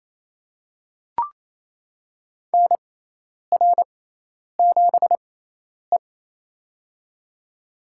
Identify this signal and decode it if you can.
Morse code, heard as a keyed tone, about 27 wpm, spaced as for 10 wpm, 700 Hz. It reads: NR7E
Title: USA Call Signs - 4 Characters - 6x Character Spacing - 28wpm